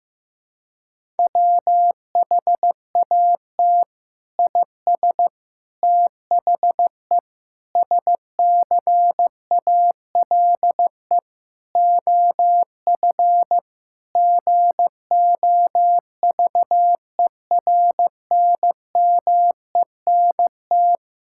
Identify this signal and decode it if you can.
Morse code, heard as a keyed tone, about 15 wpm, 700 Hz. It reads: WHAT IS THE SCALE OF GOVERNMENT
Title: Sentences from Top 1000 Words - Review Part 5 - 15wpm